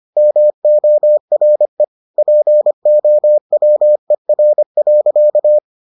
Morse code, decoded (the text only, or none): MORE POWER.